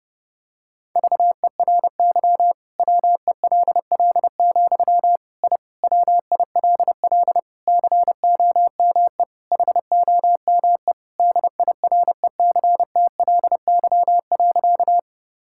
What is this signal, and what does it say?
Morse code, heard as a keyed tone, about 30 wpm, 700 Hz. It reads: VERY WELL, I WILL COME HOME DIRECTLY.
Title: Sentences from Top 1000 Words - Review Part 5 - 30wpm